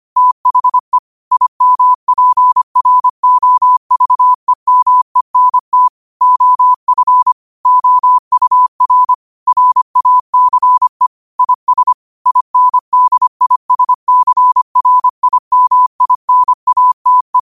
Morse code, heard as a keyed tone, about 25 wpm, 1000 Hz. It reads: THE IMPROVEMENT OF OUR RACE IS INDISCRIMINATE
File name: qotdfiles / 25wpm